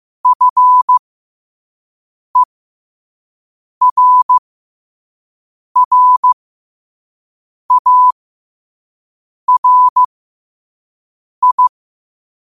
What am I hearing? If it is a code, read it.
Morse code, heard as a keyed tone, about 15 wpm, spaced as for 5 wpm, 1000 Hz. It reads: FERRARI
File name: qotdfiles / 5wpm